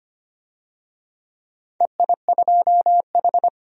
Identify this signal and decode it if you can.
Morse code, heard as a keyed tone, about 25 wpm, 700 Hz. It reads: EI2H
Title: DXCC-100 Call Signs - 4 Characters - 25wpm